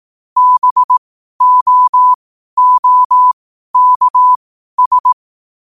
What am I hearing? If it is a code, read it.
Morse code, heard as a keyed tone, about 18 wpm, spaced as for 12 wpm, 1000 Hz. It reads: BOOKS
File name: qotdfiles / 12wpm